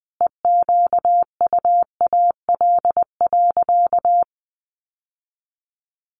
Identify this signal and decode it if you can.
Morse code, heard as a keyed tone, about 20 wpm, 700 Hz. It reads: EQUAL.
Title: Sentences from Top 1400 Words 20wpm